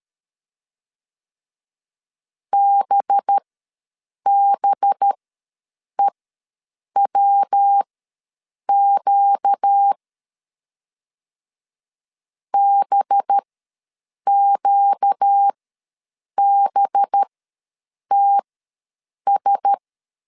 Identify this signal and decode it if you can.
Morse code, heard as a keyed tone, about 13 wpm, spaced as for 7 wpm, 800 Hz. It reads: BBEWQ BQBTS
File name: LEZIONI / 5 LBVC / 5 LB ws 12wpm 3034